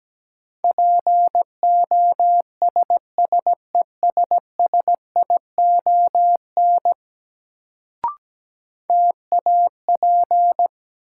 Morse code, heard as a keyed tone, about 17 wpm, 700 Hz. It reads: POSSESSION TAP
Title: Sentences from Top 1400 Words - New Word Drill Alt - 17wpm